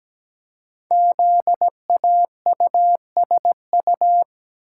Morse code, heard as a keyed tone, about 17 wpm, 700 Hz. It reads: ZAUSU
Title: Code Groups - 5 Letters - 17wpm